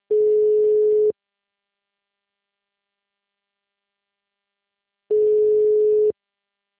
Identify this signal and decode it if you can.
Ringback tone (Europe, 425 Hz).